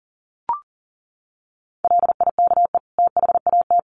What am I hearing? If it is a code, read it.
Morse code, heard as a keyed tone, about 37 wpm, 700 Hz. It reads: LIKE THAT
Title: Sets of 2 Words 40wpm